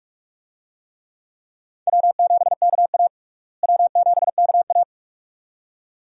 Morse code, heard as a keyed tone, about 46 wpm, 700 Hz. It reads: W7KA W7KA
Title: USA Call Signs - 4 Characters - Repeated 2x - 45wpm